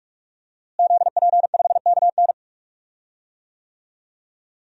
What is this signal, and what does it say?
Morse code, heard as a keyed tone, about 41 wpm, 700 Hz. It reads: ZP5KN
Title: DXCC-100 Call Signs - 5 Characters - Repeated 2x - 45wpm